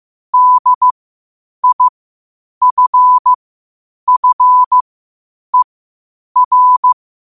Morse code, heard as a keyed tone, about 15 wpm, spaced as for 8 wpm, 1000 Hz. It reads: DIFFER
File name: qotdfiles / 8wpm